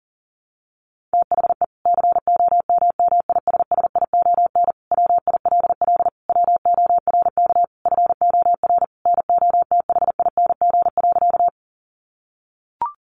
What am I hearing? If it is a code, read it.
Morse code, heard as a keyed tone, about 37 wpm, 700 Hz. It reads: THE COMMISSION WILL WORK FOR NOTHING.
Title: Sentences from Top 1400 Words 40wpm